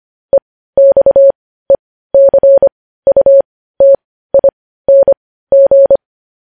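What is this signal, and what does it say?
Morse code, heard as a keyed tone, about 25 wpm, spaced as for 15 wpm, 550 Hz. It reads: EXECUTING
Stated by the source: Morse news headlines at 15 wpm.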